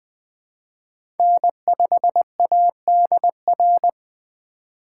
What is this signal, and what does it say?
Morse code, heard as a keyed tone, about 20 wpm, 700 Hz. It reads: N5ADR
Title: USA Call Signs - 2x Character Spacing - 20wpm